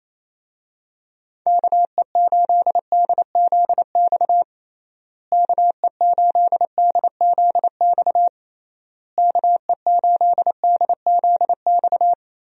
Morse code, heard as a keyed tone, about 28 wpm, 700 Hz. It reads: KE8DZX KE8DZX KE8DZX
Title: USA Call Signs - 6 Characters - Repeated 3x - 28wpm